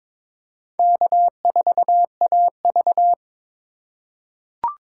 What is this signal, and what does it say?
Morse code, heard as a keyed tone, about 22 wpm, 700 Hz. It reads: K4AV
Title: USA Call Signs - 4 Characters - 2x Character Spacing - 22wpm